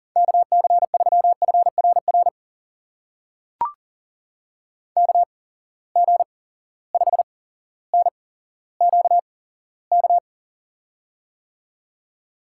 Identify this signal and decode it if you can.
Morse code, heard as a keyed tone, about 39 wpm, spaced as for 17 wpm, 700 Hz. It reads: KC3FRR KG5NQK
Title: USA Call Signs - 6 Characters - 8x Character Spacing - 40wpm